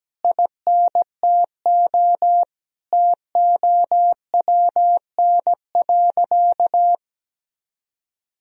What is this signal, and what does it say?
Morse code, heard as a keyed tone, about 17 wpm, 700 Hz. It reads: INTO TOWN.
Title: Sentences from Top 1000 Words - Review Part 1 - 17wpm